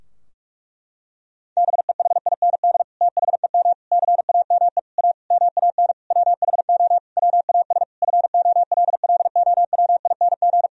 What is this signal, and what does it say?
Morse code, heard as a keyed tone, about 46 wpm, 700 Hz. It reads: BEHIND THEM CAME A MAN WHO WAS FOLLOWING
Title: Sentences from Top 1000 Words - Review Part 4 - 45wpm